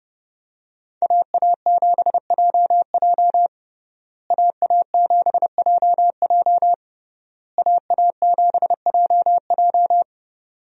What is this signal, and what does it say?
Morse code, heard as a keyed tone, about 29 wpm, 700 Hz. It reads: AA7JJ AA7JJ AA7JJ
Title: USA Call Signs - 5 Characters - Repeated 3x - 30wpm